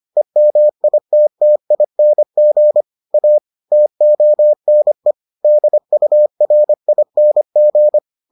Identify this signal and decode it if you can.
Morse code, heard as a keyed tone, about 25 wpm, 600 Hz. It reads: EMITTING A TONE DURING